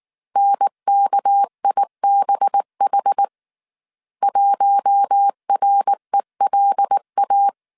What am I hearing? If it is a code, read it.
Morse code, heard as a keyed tone, about 19 wpm, 800 Hz. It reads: NKIBH 1RELA
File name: CW / DET / 1 6 1500C S A 18wpm 2255